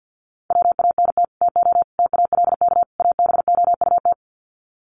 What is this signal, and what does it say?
Morse code, heard as a keyed tone, about 50 wpm, 700 Hz. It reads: WANT TO TALK ABOUT
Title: Sets of 4 Words 50wpm